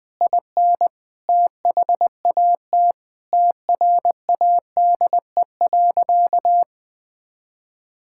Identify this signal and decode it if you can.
Morse code, heard as a keyed tone, about 21 wpm, 700 Hz. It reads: IN THAT TRADE.
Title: Sentences from Top 1000 Words - Review Part 5 - 20wpm